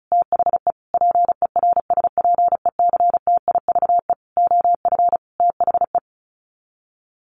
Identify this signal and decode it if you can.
Morse code, heard as a keyed tone, about 34 wpm, 700 Hz. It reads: THE PERSPECTIVE OF THE
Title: Sets of 5 Words Encore2 35wpm